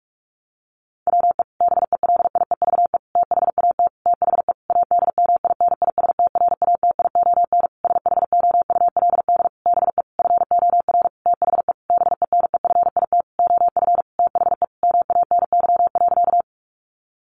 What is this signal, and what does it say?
Morse code, heard as a keyed tone, about 44 wpm, 700 Hz. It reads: WE BELIEVE THAT THE ADMINISTRATION SHOULD BE FOR THE BENEFIT OF THE MANY.
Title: Sentences from Top 1300 Words 45wpm